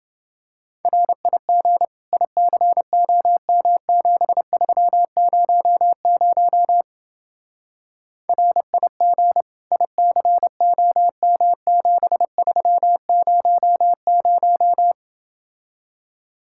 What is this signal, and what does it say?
Morse code, heard as a keyed tone, about 28 wpm, 700 Hz. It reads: RIG ICOM7300 RIG ICOM7300